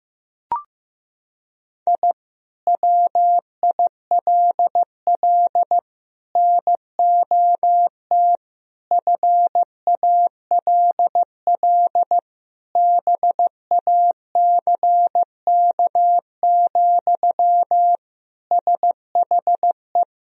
Morse code, heard as a keyed tone, about 15 wpm, 700 Hz. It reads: I WILL NOT FALL BACK, SHE
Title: Sentences from Top 1000 Words - Review Part 2 - 15wpm